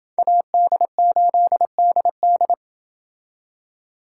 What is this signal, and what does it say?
Morse code, heard as a keyed tone, about 28 wpm, 700 Hz. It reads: AD8DD